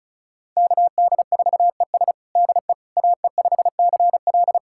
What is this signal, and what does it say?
Morse code, heard as a keyed tone, about 36 wpm, 700 Hz. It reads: KD4ES DE AE5CL
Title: USA Callsign DE Callsign - 5 Characters - 35wpm